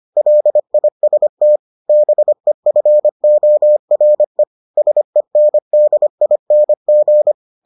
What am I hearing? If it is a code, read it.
Morse code, heard as a keyed tone, about 25 wpm, 600 Hz. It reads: LIST BEFORE SENDING